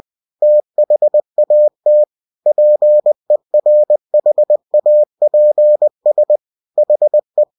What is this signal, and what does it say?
Morse code, heard as a keyed tone, about 20 wpm, 600 Hz. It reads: THAT PERHAPS HE